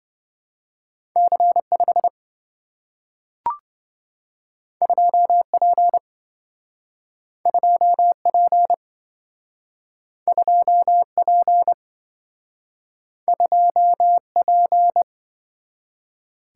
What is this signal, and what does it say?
Morse code, heard as a keyed tone, about 27 wpm, 700 Hz. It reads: C5 2P 2P 2P 2P